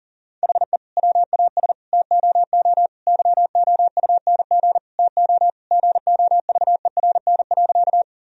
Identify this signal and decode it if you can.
Morse code, heard as a keyed tone, about 40 wpm, 700 Hz. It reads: HE WAS TOO YOUNG TO GOVERN.
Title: Sentences from Top 1000 Words - Review Part 2 - 40wpm